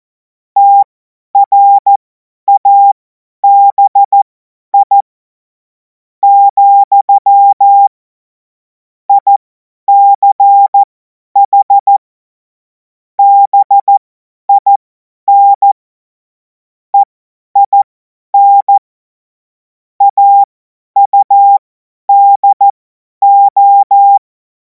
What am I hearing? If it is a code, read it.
Morse code, heard as a keyed tone, about 14 wpm, spaced as for 9 wpm, 800 Hz. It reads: TRABI, ICH BIN EIN AUDO
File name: Morsekurs / Witze / Witz 30. Juni 2026 50 F70